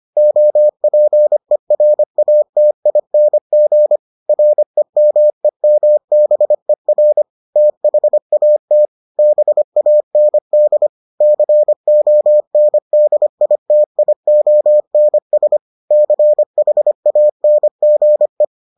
Morse code, heard as a keyed tone, about 25 wpm, 600 Hz. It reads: OPERATING REMEMBER THAT BAND CONDITIONS CHANGE